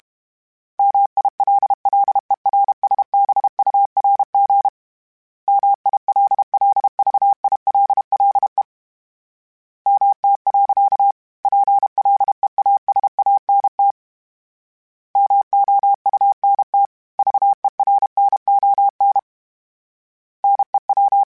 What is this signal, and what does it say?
Morse code, heard as a keyed tone, about 32 wpm, 800 Hz. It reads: MILLERSBURG MILLVILLE MT. PLEASANT MOUNT VERNON NEW
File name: fromSMs / FromSMs mp3s / Cities@30